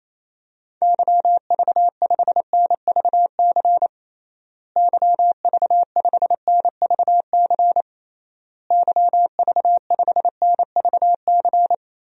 Morse code, heard as a keyed tone, about 28 wpm, 700 Hz. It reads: YV5NVC YV5NVC YV5NVC
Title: DXCC-100 Call Signs - 6 Characters - Repeated 3x - 28wpm